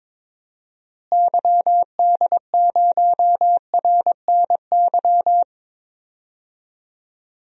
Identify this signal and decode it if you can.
Morse code, heard as a keyed tone, about 21 wpm, 700 Hz. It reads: YD0RNY
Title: DXCC-100 Call Signs - 6 Characters - Repeated 3x - 22wpm